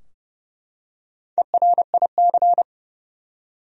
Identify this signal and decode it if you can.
Morse code, heard as a keyed tone, about 31 wpm, 700 Hz. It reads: ERIC
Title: USA Common Names - Max 5 Letters - 30wpm